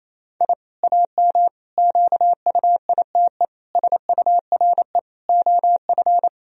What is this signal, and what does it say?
Morse code, heard as a keyed tone, about 28 wpm, 700 Hz. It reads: I AM QUITE SURE OF
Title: Sentences from Top 1000 Words - Review Part 2 - 28wpm